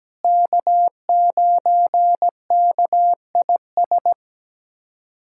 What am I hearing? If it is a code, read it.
Morse code, heard as a keyed tone, about 17 wpm, 700 Hz. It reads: K9KIS
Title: USA Callsign DE Callsign - 5 Characters - 17wpm